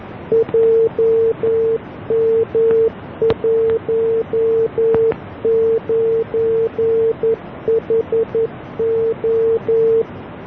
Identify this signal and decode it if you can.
Morse code, heard as a keyed tone, about 11 wpm, 450 Hz. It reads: JM19HO